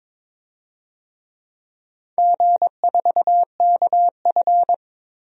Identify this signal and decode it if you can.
Morse code, heard as a keyed tone, about 22 wpm, 690 Hz. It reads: G4KF
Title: DXCC-100 Call Signs - 4 Characters - 22wpm